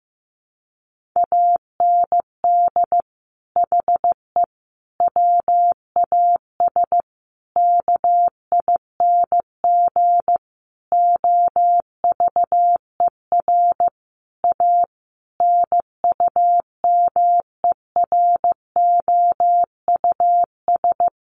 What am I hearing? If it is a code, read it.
Morse code, heard as a keyed tone, about 15 wpm, 700 Hz. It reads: AND HE WAS KING OVER A NUMEROUS
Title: Sentences from Top 1400 Words 15wpm